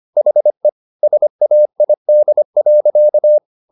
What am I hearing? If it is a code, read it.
Morse code, heard as a keyed tone, about 25 wpm, 600 Hz. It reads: HE SAID.
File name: the old man and the sea part 1 25wpm